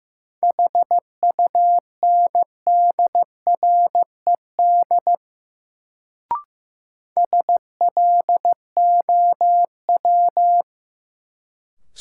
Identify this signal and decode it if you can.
Morse code, heard as a keyed tone, about 15 wpm, 700 Hz. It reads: HUNDRED SLOW
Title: Sentences from Top 400 Words - New Word Drill - 15wpm